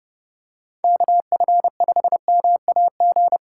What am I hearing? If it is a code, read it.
Morse code, heard as a keyed tone, about 32 wpm, 700 Hz. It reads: KF5MAG